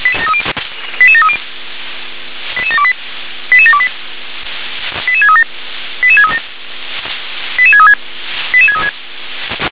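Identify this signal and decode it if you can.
Five-tone selective call (ZVEI), heard as quick runs of five tones, tones of about 70 ms. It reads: 88730 88728 88728 88637 88637 88636 88636
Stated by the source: ZVEI five-tone selective call received on an SDR.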